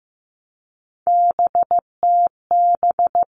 Morse code, heard as a keyed tone, about 15 wpm, 700 Hz. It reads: BTB
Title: All Cut Numbers - 3-Digit 15wpm